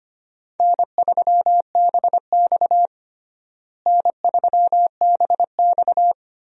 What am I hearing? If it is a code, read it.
Morse code, heard as a keyed tone, about 25 wpm, 700 Hz. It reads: N3BX N3BX
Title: USA Call Signs - 4 Characters - Repeated 2x - 25wpm